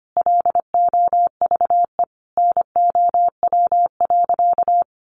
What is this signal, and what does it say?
Morse code, heard as a keyed tone, about 25 wpm, 700 Hz. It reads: LOVE NOW.
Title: Sentences from Top 1300 Words 25wpm